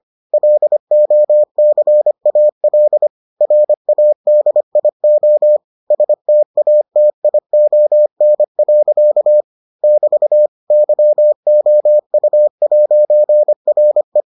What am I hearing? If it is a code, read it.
Morse code, heard as a keyed tone, about 25 wpm, 600 Hz. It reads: LOCAL RADIO STATION. <BT> YOU'RE